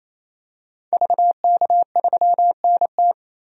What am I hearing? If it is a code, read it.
Morse code, heard as a keyed tone, about 28 wpm, 700 Hz. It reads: VK3NT
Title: DXCC-100 Call Signs - 5 Characters - 28wpm